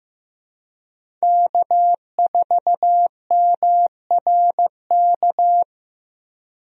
Morse code, heard as a keyed tone, about 16 wpm, 700 Hz. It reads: K4MRK
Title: USA Call Signs - 5 Characters - 15wpm